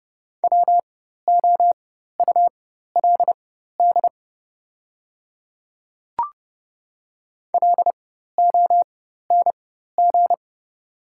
Morse code, heard as a keyed tone, about 30 wpm, spaced as for 14 wpm, 700 Hz. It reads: WOULD LONG
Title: 4x Character Spacing - Top 100 Words - 30wpm